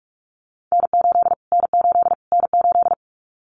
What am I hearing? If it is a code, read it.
Morse code, heard as a keyed tone, about 44 wpm, 700 Hz. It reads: N8 N8 N8
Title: Callsign Prefixes for Top 25 DXCC - Repeated 3x - 45wpm